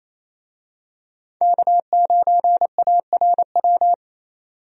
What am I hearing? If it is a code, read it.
Morse code, heard as a keyed tone, about 28 wpm, 700 Hz. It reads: K9ARW
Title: USA Call Signs - 5 Characters - 28wpm